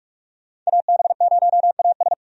Morse code, heard as a keyed tone, about 45 wpm, 700 Hz. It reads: AB0AS
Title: USA Call Signs - 5 Characters - 45wpm